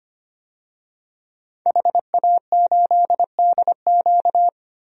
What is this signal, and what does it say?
Morse code, heard as a keyed tone, about 25 wpm, 700 Hz. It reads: HA8DQ
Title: DXCC-100 Call Signs - 5 Characters - Repeated 3x - 25wpm